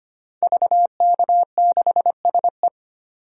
Morse code, heard as a keyed tone, about 25 wpm, 700 Hz. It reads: VK6SE
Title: DXCC-100 Call Signs - 5 Characters - 25wpm